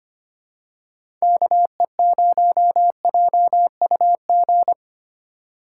Morse code, heard as a keyed tone, about 25 wpm, 700 Hz. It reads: KE0JUG
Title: USA Call Signs - 6 Characters - 2x Character Spacing - 25wpm